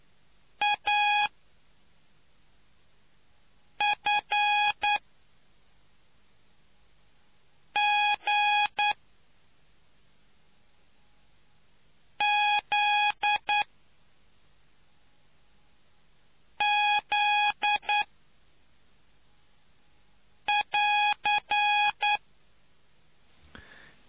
Morse code, heard as a keyed tone, about 9 wpm, spaced as for 3 wpm, 830 Hz. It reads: AFGZZ <AR>